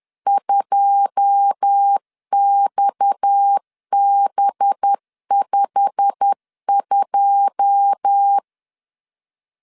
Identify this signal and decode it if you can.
Morse code, heard as a keyed tone, about 10 wpm, 800 Hz. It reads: 2XB52
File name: LEZIONI / 9 2 10wpm 2736